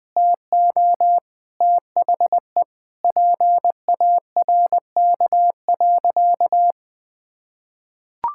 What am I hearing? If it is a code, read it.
Morse code, heard as a keyed tone, about 20 wpm, 700 Hz. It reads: TO THE PARK.
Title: Sentences with 3-Word Phrases - Spoken Beginning - 20wpm